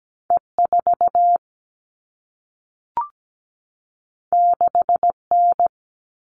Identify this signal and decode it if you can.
Morse code, heard as a keyed tone, about 17 wpm, 700 Hz. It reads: E4 6N
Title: All Cut Numbers - 2-Digit 17wpm